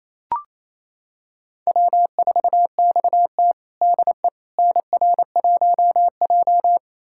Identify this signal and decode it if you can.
Morse code, heard as a keyed tone, about 29 wpm, 700 Hz. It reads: W4XT DE NR1J
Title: USA Callsign DE Callsign - 4 Characters - 28wpm